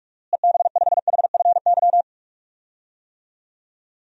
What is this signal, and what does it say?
Morse code, heard as a keyed tone, about 41 wpm, 700 Hz. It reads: EB5HFY
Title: DXCC-100 Call Signs - 6 Characters - Repeated 2x - 45wpm